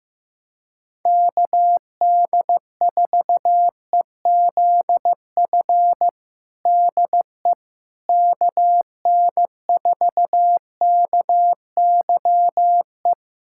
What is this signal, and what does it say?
Morse code, heard as a keyed tone, about 15 wpm, 700 Hz. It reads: KD4EZF DE KN4KYE
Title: USA Callsign DE Callsign 15wpm